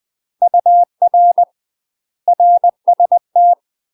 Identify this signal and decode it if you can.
Morse code, heard as a keyed tone, about 20 wpm, 700 Hz. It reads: UR RST
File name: UK MASTER QSOs TRAINING / 20WPM / UK MASTER QSOs 0059